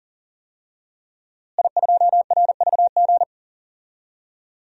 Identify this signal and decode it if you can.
Morse code, heard as a keyed tone, about 40 wpm, 700 Hz. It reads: I2RVG
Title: DXCC-100 Call Signs - 5 Characters - Repeated 3x - 40wpm